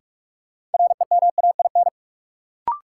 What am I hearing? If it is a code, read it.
Morse code, heard as a keyed tone, about 45 wpm, 700 Hz. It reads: REMAIN
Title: Sentences from Top 1200 Words - New Word Drill - 45wpm